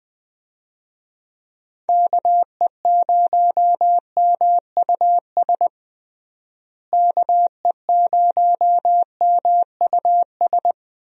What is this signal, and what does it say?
Morse code, heard as a keyed tone, about 20 wpm, 700 Hz. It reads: KE0MUS KE0MUS
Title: USA Call Signs - 6 Characters - Repeated 3x - 20wpm